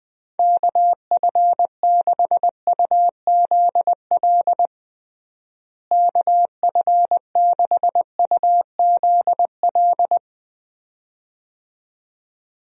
Morse code, heard as a keyed tone, about 20 wpm, 700 Hz. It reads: KF6UZL KF6UZL
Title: USA Call Signs - 6 Characters - Repeated 2x - 20wpm